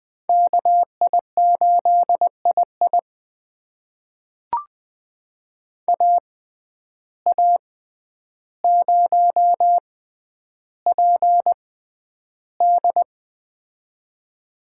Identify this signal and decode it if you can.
Morse code, heard as a keyed tone, about 20 wpm, spaced as for 10 wpm, 700 Hz. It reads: KI8II AA0PD
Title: USA Call Signs - 5 Characters - 6x Character Spacing - 20wpm